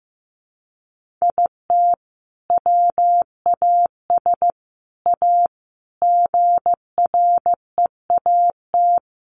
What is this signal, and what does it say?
Morse code, heard as a keyed tone, about 15 wpm, 700 Hz. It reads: IT WAS A GREAT
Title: Sentences from Top 1500 Words 15wpm